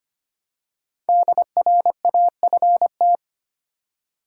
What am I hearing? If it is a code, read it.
Morse code, heard as a keyed tone, about 25 wpm, 700 Hz. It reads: DRAFT
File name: 5 Letter Words - First 3 Letters Given - 25wpm